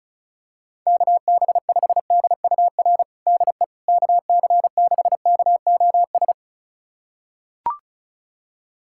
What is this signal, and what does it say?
Morse code, heard as a keyed tone, about 36 wpm, 700 Hz. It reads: KB5DUR DE KC6KOS
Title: USA Callsign DE Callsign - 6 Characters - 35wpm